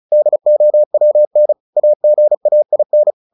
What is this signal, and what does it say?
Morse code, heard as a keyed tone, about 34 wpm, 600 Hz. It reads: DOWN AGAIN
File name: the old man and the sea 2 35wpm